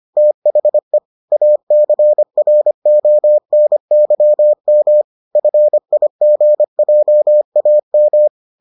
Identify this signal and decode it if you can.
Morse code, heard as a keyed tone, about 25 wpm, 600 Hz. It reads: THE ACRONYM FIGJAM